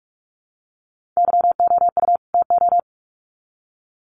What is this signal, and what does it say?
Morse code, heard as a keyed tone, about 44 wpm, 700 Hz. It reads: YOU TO
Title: Sets of 2 Words 45wpm